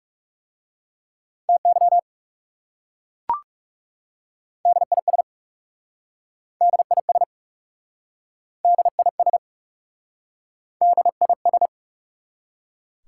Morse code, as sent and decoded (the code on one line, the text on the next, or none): - -.-- / -.. .. ... / -.. .. ... / -.. .. ... / -.. .. ...
TY DIS DIS DIS DIS